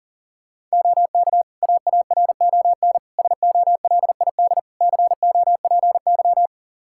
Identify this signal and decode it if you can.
Morse code, heard as a keyed tone, about 40 wpm, 700 Hz. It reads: OK AARON SOLID COPY
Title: K1USN SST QSO Practice 40wpm